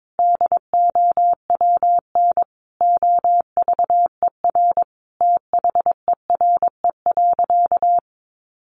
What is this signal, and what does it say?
Morse code, heard as a keyed tone, about 22 wpm, 700 Hz. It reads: DOWN OVER THERE.
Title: Sentences from Top 800 Words 22wpm